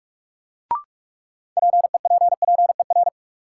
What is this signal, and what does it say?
Morse code, heard as a keyed tone, about 46 wpm, 700 Hz. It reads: PEPPER
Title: Sets of 5 Words - New Word Drill - 45wpm